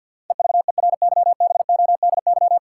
Morse code, heard as a keyed tone, about 50 wpm, 700 Hz. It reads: EVERYBODY
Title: Sentences from Top 1400 Words - New Word Drill - 50wpm